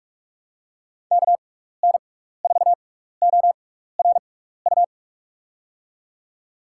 Morse code, heard as a keyed tone, about 46 wpm, spaced as for 15 wpm, 700 Hz. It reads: KN4ORU
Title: USA Call Signs - 6 Characters - 6x Character Spacing - 45wpm